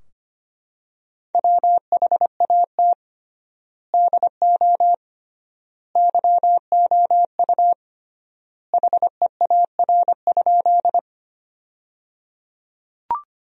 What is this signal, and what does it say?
Morse code, heard as a keyed tone, about 25 wpm, 700 Hz. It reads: WHAT DO YOU HEAR?
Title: 3x Word-Spacing - Sentences from Top 300 Words - 25wpm